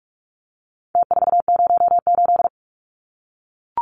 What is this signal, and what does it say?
Morse code, heard as a keyed tone, about 44 wpm, 700 Hz. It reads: T408
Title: Numbers - Contest Style - 4-Digit 45wpm